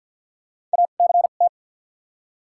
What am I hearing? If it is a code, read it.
Morse code, heard as a keyed tone, about 48 wpm, spaced as for 35 wpm, 700 Hz. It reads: ACT
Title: 2x Character Spacing - Top 300 Words - 50wpm